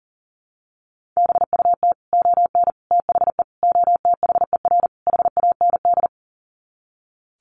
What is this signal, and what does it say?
Morse code, heard as a keyed tone, about 40 wpm, 700 Hz. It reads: BUT ON THE OTHER HAND
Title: Sets of 5 Words 40wpm